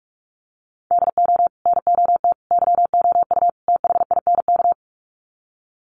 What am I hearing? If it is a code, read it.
Morse code, heard as a keyed tone, about 44 wpm, 700 Hz. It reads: DO NOT YOU THINK
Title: Sets of 4 Words 45wpm